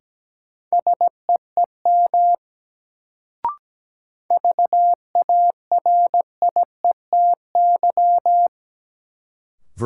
Morse code, heard as a keyed tone, about 17 wpm, 700 Hz. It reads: SEEM VARIETY
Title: Sets of 3 Words - New Word Drill - 17wpm